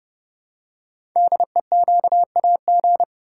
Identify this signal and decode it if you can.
Morse code, heard as a keyed tone, about 30 wpm, 700 Hz. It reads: DEQAG
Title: Code Groups - 5 Letters - 30wpm